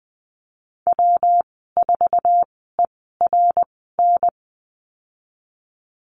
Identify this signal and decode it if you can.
Morse code, heard as a keyed tone, about 20 wpm, spaced as for 14 wpm, 700 Hz. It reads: W4ERN
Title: 140 Lesson 23g - Letter f - Call Signs 20wpm